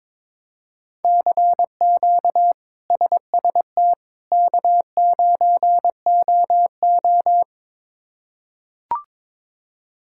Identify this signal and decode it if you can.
Morse code, heard as a keyed tone, about 21 wpm, 700 Hz. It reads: CQ SST K9OO
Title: K1USN SST QSO Practice - 22wpm